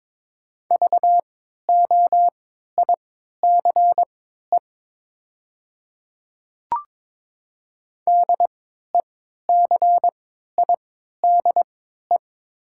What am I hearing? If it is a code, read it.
Morse code, heard as a keyed tone, about 22 wpm, spaced as for 12 wpm, 700 Hz. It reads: VOICE DECIDE
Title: 3x Character Spacing - Top 500 Words - 22wpm